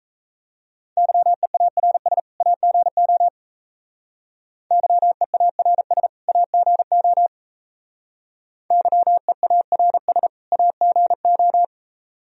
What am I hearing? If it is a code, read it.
Morse code, heard as a keyed tone, about 34 wpm, 700 Hz. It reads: YEARS AGO YEARS AGO YEARS AGO
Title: Speed-Racing - Sets of 2 Words 28wpm